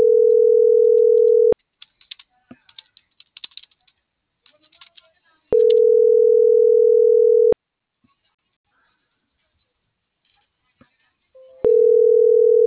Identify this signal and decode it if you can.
Ringback tone (North America), 440 + 480 Hz.